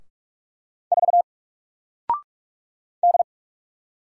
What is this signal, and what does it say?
Morse code, heard as a keyed tone, about 48 wpm, 700 Hz. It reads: 4 D
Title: Single Letter-Number Alt 45wpm